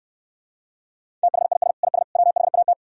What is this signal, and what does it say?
Morse code, heard as a keyed tone, about 69 wpm, 700 Hz. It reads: THIS IS WHAT